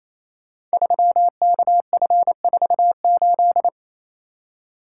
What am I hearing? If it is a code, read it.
Morse code, heard as a keyed tone, about 27 wpm, 700 Hz. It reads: 3KF48
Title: Code Groups - 5 Letter:Numbers - 28wpm